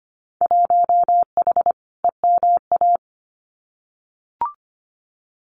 Morse code, heard as a keyed tone, about 25 wpm, 700 Hz. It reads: 1H EMA
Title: Winter Field Day Exchanges 25wpm